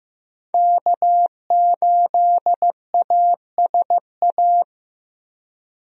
Morse code, heard as a keyed tone, about 15 wpm, 700 Hz. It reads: K8ASA
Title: USA Call Signs - 5 Characters - Repeated 3x - 15wpm